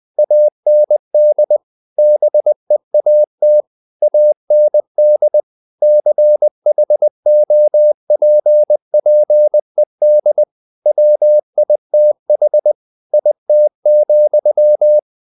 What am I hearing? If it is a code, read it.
Morse code, heard as a keyed tone, about 20 wpm, 600 Hz. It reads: AND BEAT AND CHOPPED WITH IT,
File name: the old man and the sea part 2 20wpm